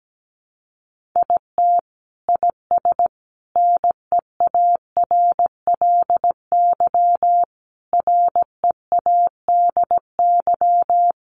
Morse code, heard as a keyed tone, about 17 wpm, 700 Hz. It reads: IT IS NEARLY READY
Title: Sentences from Top 1500 Words 17wpm